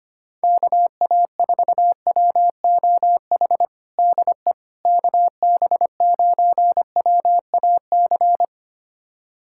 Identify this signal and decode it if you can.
Morse code, heard as a keyed tone, about 25 wpm, 700 Hz. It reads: KA4WOH DE KB9WAC
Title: USA Callsign DE Callsign 25wpm